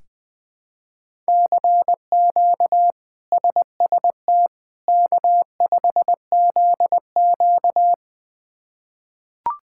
Morse code, heard as a keyed tone, about 20 wpm, 700 Hz. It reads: CQ SST K5ZQ
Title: K1USN SST QSO Practice - 20wpm